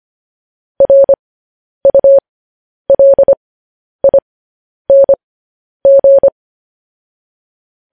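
Morse code, heard as a keyed tone, about 25 wpm, spaced as for 10 wpm, 550 Hz. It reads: RULING
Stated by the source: Morse code news headlines practice, at 10 WPM.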